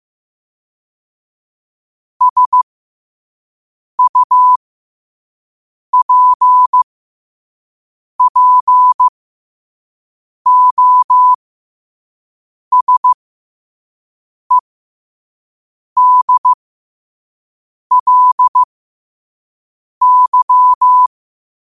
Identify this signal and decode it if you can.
Morse code, heard as a keyed tone, about 15 wpm, spaced as for 5 wpm, 1000 Hz. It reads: SUPPOSEDLY
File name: qotdfiles / 5wpm